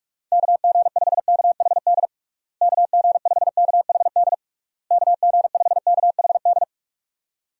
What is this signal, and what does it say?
Morse code, heard as a keyed tone, about 46 wpm, 700 Hz. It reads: KG5KHD KG5KHD KG5KHD
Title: USA Call Signs - 6 Characters - Repeated 3x - 45wpm